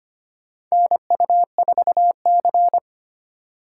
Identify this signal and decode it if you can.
Morse code, heard as a keyed tone, about 25 wpm, 700 Hz. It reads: NU4C
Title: USA Call Signs - 4 Characters - 4x Character Spacing - 25wpm